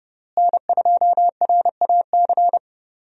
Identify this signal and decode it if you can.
Morse code, heard as a keyed tone, about 30 wpm, 700 Hz. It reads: N2RAC
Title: USA Call Signs - 5 Characters - Repeated 3x - 30wpm